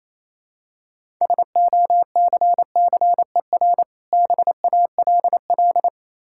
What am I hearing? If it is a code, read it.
Morse code, heard as a keyed tone, about 27 wpm, 700 Hz. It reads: SOCCER BALL
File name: Easy Words - Speed-Racing with Hints - 22wpm